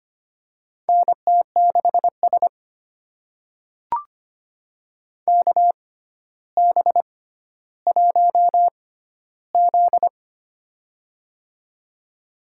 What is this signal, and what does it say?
Morse code, heard as a keyed tone, about 25 wpm, spaced as for 13 wpm, 700 Hz. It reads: NT6S KB1Z
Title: USA Call Signs - 4 Characters - 6x Character Spacing - 25wpm